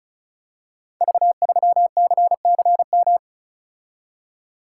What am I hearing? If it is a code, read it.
Morse code, heard as a keyed tone, about 36 wpm, 690 Hz. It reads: V3CCM
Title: Code Groups - 5 Letter:Numbers - 35wpm